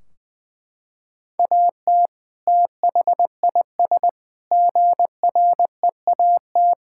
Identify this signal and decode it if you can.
Morse code, heard as a keyed tone, about 20 wpm, 700 Hz. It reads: AT THIS GREAT
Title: Sentences from Top 1600 Words 20wpm